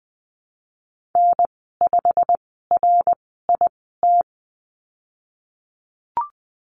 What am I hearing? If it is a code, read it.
Morse code, heard as a keyed tone, about 20 wpm, spaced as for 15 wpm, 700 Hz. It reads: N5RIT
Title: 088 Lesson 15g - Number 5 - Call Signs 20wpm